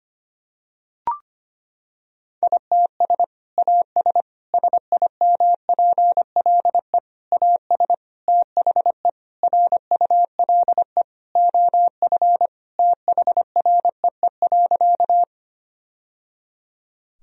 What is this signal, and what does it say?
Morse code, heard as a keyed tone, about 25 wpm, 700 Hz. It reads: ITS AS SIMPLE AS THE RULE OF THREE.
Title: Sentences from Top 1000 Words - Review Part 2 - 25wpm